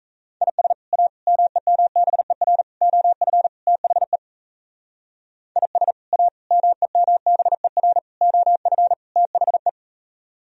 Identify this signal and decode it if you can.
Morse code, heard as a keyed tone, about 36 wpm, 700 Hz. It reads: IS A MEMBER OF THE IS A MEMBER OF THE